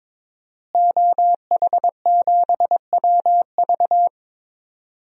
Morse code, heard as a keyed tone, about 22 wpm, 700 Hz. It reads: OH7WV
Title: DXCC-100 Call Signs - 5 Characters - Repeated 3x - 22wpm